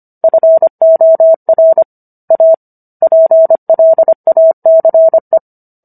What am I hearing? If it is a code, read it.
Morse code, heard as a keyed tone, about 25 wpm, 650 Hz. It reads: FOR A PLACE